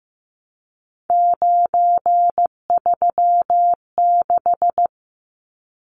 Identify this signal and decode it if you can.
Morse code, heard as a keyed tone, about 15 wpm, 700 Hz. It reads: 936
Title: Numbers - 3-Digit 15wpm